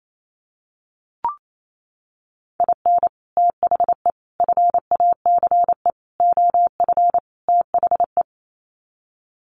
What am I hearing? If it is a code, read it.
Morse code, heard as a keyed tone, about 28 wpm, 700 Hz. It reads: IN THE FACE OF THE
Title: Sets of 5 Words 28wpm